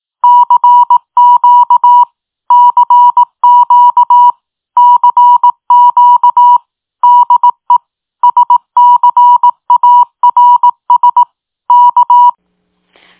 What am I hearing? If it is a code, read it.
Morse code, heard as a keyed tone, about 18 wpm, 970 Hz. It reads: CQ CQ CQ DE SCARS K
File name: tin-upgrade / QSO 2023 04 20 .mp3